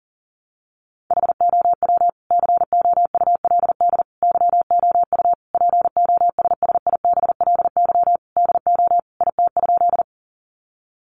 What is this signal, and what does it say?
Morse code, heard as a keyed tone, about 40 wpm, 700 Hz. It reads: HOW COULD YOU POSSIBLY DO IT?
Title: Sentences from Top 1300 Words 40wpm